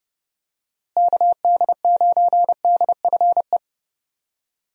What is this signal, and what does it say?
Morse code, heard as a keyed tone, about 30 wpm, 700 Hz. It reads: KD9DFE
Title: USA Call Signs - 6 Characters - Repeated 2x - 30wpm